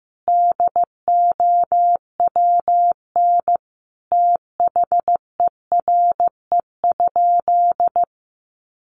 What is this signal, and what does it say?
Morse code, heard as a keyed tone, about 15 wpm, 700 Hz. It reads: DOWN THERE?
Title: Sentences from Top 1100 Words 15wpm